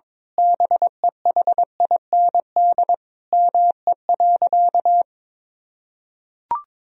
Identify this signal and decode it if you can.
Morse code, heard as a keyed tone, about 22 wpm, 700 Hz. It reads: BEHIND ME.
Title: Sentences from Top 1000 Words - Review Part 1 - 22wpm